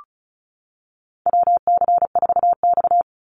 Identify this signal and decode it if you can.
Morse code, heard as a keyed tone, about 34 wpm, 700 Hz. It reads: WC4X
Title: Call Signs - 35wpm